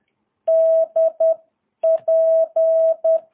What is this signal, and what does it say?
Morse code, heard as a keyed tone, about 10 wpm, spaced as for 9 wpm, 650 Hz. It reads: DP
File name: jh8cbh / 2022cw